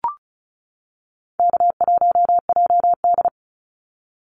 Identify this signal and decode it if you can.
Morse code, heard as a keyed tone, about 34 wpm, 700 Hz. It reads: K1JD
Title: NA SOTA Activator Callsigns 35wpm